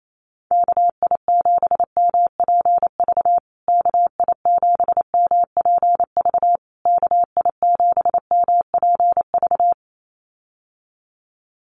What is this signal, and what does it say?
Morse code, heard as a keyed tone, about 28 wpm, 700 Hz. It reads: KI7MPV KI7MPV KI7MPV
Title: USA Call Signs Repeated 3x 28wpm